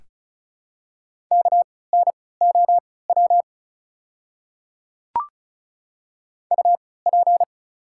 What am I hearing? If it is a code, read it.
Morse code, heard as a keyed tone, about 36 wpm, spaced as for 20 wpm, 690 Hz. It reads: KNOW UP
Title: 3x Character Spacing - Top 200 Words - 35wpm